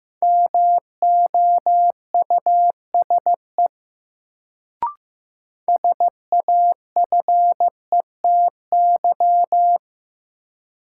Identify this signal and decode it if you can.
Morse code, heard as a keyed tone, about 15 wpm, 700 Hz. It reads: MOUSE SAFETY
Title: Sentences from Top 700 Words - New Word Drill - 15wpm